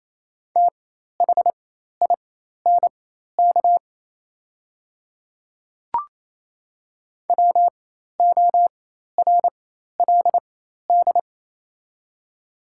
Morse code, heard as a keyed tone, about 28 wpm, spaced as for 13 wpm, 690 Hz. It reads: THINK WORLD
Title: 4x Character Spacing - Top 200 Words - 28wpm